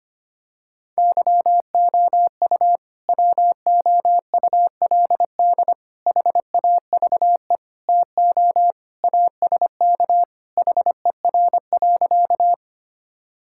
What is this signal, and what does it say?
Morse code, heard as a keyed tone, about 25 wpm, 700 Hz. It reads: YOU WOULD HAVE TO ASK HER.
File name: practice / 25wpm / Sentences with 3-Word Phrases - 25wpm